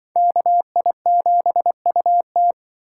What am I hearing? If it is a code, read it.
Morse code, heard as a keyed tone, about 24 wpm, 700 Hz. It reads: KI7UT
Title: Speed Racing - USA Call Signs 20wpm